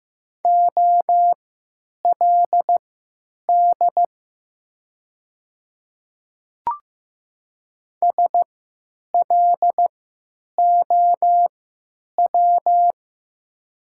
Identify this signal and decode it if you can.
Morse code, heard as a keyed tone, about 15 wpm, spaced as for 9 wpm, 700 Hz. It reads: OLD SLOW
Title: 3x Character Spacing - Top 500 Words - 15wpm